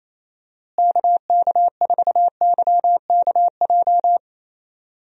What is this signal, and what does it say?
Morse code, heard as a keyed tone, about 28 wpm, 700 Hz. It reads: KK4YKJ
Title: USA Call Signs - 6 Characters - Repeated 2x - 28wpm